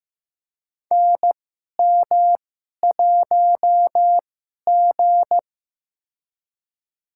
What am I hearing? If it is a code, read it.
Morse code, heard as a keyed tone, about 15 wpm, spaced as for 11 wpm, 700 Hz. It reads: NM1G
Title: USA Call Signs - 4 Characters - 2x Character Spacing - 15wpm